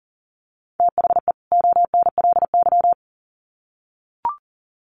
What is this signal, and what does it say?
Morse code, heard as a keyed tone, about 40 wpm, 700 Hz. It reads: THE ONLY
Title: Sets of 2 Words 40wpm